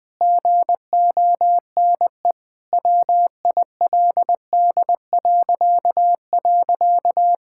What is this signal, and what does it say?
Morse code, heard as a keyed tone, about 20 wpm, 700 Hz. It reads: GONE WILD..
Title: Pangram Challenge v2 - 20wpm